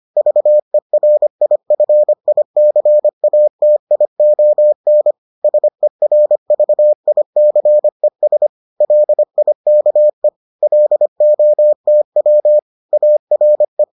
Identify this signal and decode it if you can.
Morse code, heard as a keyed tone, about 25 wpm, 600 Hz. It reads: VERIFICATION SERVICES LIKE LOTW ARE